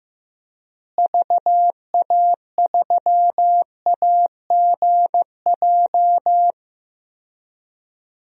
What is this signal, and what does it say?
Morse code, heard as a keyed tone, about 15 wpm, 700 Hz. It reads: VA3AGJ
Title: DXCC-100 Call Signs - 6 Characters - Repeated 3x - 15wpm